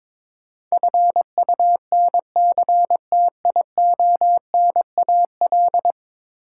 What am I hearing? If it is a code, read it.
Morse code, heard as a keyed tone, about 22 wpm, 700 Hz. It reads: FUNCTIONAL
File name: practice / 22wpm / 10 Letter Words - First 3 Letters Given - 22wpm